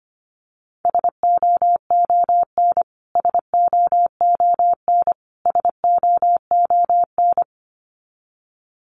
Morse code, heard as a keyed tone, about 25 wpm, 700 Hz. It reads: SOON SOON SOON
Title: Top 500 Words Repeated 25wpm